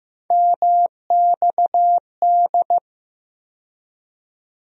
Morse code, heard as a keyed tone, about 15 wpm, 700 Hz. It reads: MXD
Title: USA Call Sign Postfixes - 3 Letters 15wpm